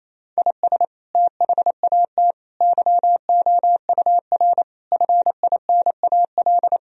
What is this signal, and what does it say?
Morse code, heard as a keyed tone, about 28 wpm, 700 Hz. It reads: IS THAT YOUR FINAL
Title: Sentences from Top 1000 Words - Review Part 2 - 28wpm